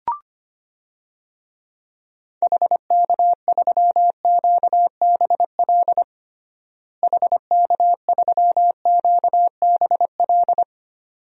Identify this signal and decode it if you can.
Morse code, heard as a keyed tone, about 25 wpm, 700 Hz. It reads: HK3QBL HK3QBL
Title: DXCC-100 Call Signs - 6 Characters - Repeated 2x - 25wpm